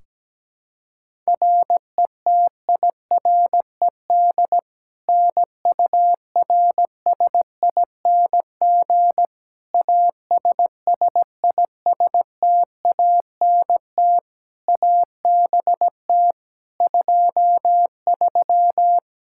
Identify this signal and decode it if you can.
Morse code, heard as a keyed tone, about 17 wpm, 700 Hz. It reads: RETIRED NURSING ASSISTANT ABT 23
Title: Practice QSO - Exchange 3 - More Info - 17wpm